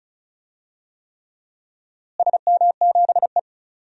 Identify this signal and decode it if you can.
Morse code, heard as a keyed tone, about 36 wpm, 700 Hz. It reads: SM7E
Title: DXCC-100 Call Signs - 4 Characters - Repeated 3x - 35wpm